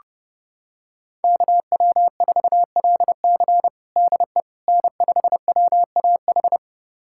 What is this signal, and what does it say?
Morse code, heard as a keyed tone, about 32 wpm, 700 Hz. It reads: KW4LC DE N5WAH
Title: USA Callsign DE Callsign - 5 Characters - 30wpm